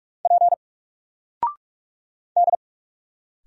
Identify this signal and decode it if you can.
Morse code, heard as a keyed tone, about 46 wpm, 700 Hz. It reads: P D
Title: Single Character (Letter:Number:Punctuation:Prosign) 45wpm